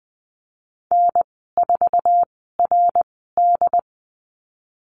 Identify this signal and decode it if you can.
Morse code, heard as a keyed tone, about 20 wpm, spaced as for 15 wpm, 700 Hz. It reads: N4RD
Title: 076 Lesson 13g - Letter L - Call Signs 20wpm